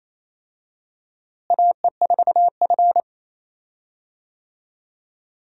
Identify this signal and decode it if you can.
Morse code, heard as a keyed tone, about 28 wpm, 700 Hz. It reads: AE4F
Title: USA Call Signs - 4 Characters - Repeated 3x - 28wpm